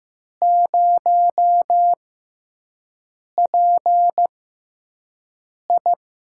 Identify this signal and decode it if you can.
Morse code, heard as a keyed tone, about 15 wpm, spaced as for 5 wpm, 700 Hz. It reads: 0PI
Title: USA Call Signs - 5 Characters - 6x Character Spacing - 15wpm